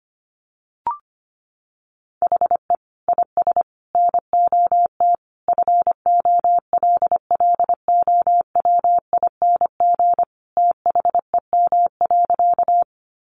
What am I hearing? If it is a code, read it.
Morse code, heard as a keyed tone, about 25 wpm, 700 Hz. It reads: HE IS NOT FOLLOWING THEM.
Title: Sentences from Top 700 Words 25wpm